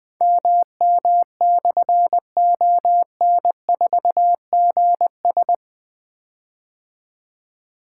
Morse code, heard as a keyed tone, about 20 wpm, 700 Hz. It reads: MM/ON4GS
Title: Difficult Call Signs 20wpm